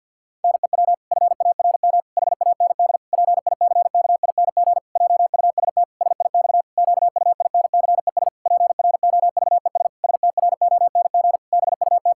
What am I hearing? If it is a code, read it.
Morse code, heard as a keyed tone, about 50 wpm, 700 Hz. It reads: NEW FARM HAND PICKING JUST SIX QUINCES PROVES STRONG BUT
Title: Pangram Challenge v2 - 50wpm